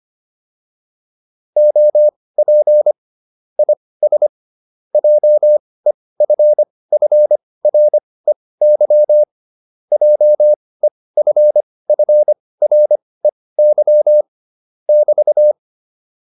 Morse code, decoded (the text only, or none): OP IS JEFFREY JEFFREY <BT>